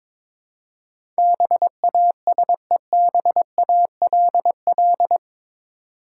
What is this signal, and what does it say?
Morse code, heard as a keyed tone, about 22 wpm, 700 Hz. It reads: BASEBALL
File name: practice / 22wpm / Eight Letter Words - 22wpm